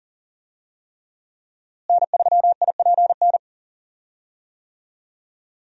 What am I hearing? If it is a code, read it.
Morse code, heard as a keyed tone, about 40 wpm, 700 Hz. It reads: N3IPN
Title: USA Call Signs - 5 Characters - Repeated 3x - 40wpm